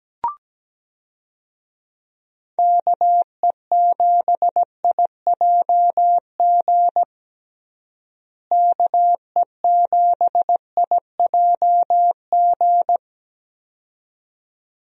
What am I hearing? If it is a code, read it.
Morse code, heard as a keyed tone, about 17 wpm, 700 Hz. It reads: KE7IJG KE7IJG
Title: USA Call Signs - 6 Characters - Repeated 2x - 17wpm